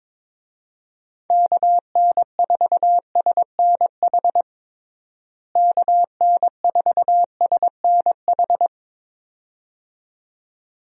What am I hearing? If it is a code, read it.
Morse code, heard as a keyed tone, about 22 wpm, 700 Hz. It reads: KN4SNH KN4SNH
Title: USA Call Signs - 6 Characters - Repeated 2x - 22wpm